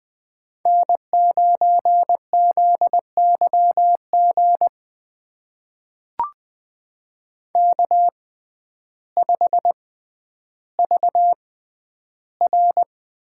Morse code, heard as a keyed tone, about 20 wpm, spaced as for 11 wpm, 700 Hz. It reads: N9ZYG K5VR
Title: USA Call Signs - 5 Characters - 6x Character Spacing - 20wpm